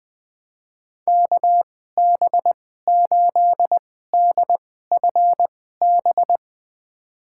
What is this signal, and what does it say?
Morse code, heard as a keyed tone, about 21 wpm, spaced as for 15 wpm, 700 Hz. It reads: KB8DFB
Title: USA Call Signs - 2x Character Spacing - 20wpm